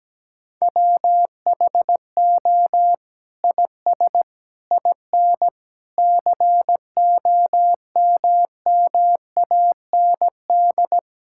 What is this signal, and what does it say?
Morse code, heard as a keyed tone, about 17 wpm, 700 Hz. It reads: WHO IS IN COMMAND
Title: Sentences from Top 1600 Words 17wpm